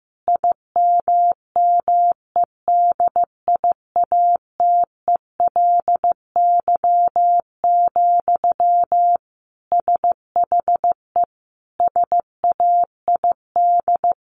Morse code, heard as a keyed tone, about 14 wpm, 700 Hz. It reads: IMMEDIATELY, SHE SAID
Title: Sentences from Top 1500 Words 15wpm